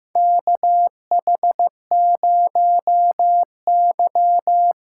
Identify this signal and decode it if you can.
Morse code, heard as a keyed tone, about 15 wpm, 700 Hz. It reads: KH0Y
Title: USA Call Signs - 5 Characters - 6x Character Spacing - 15wpm